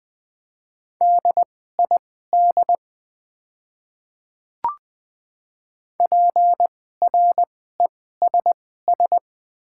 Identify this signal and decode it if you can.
Morse code, heard as a keyed tone, about 20 wpm, spaced as for 14 wpm, 700 Hz. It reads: DID PRESS
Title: 2x Character Spacing - Top 300 Words - 20wpm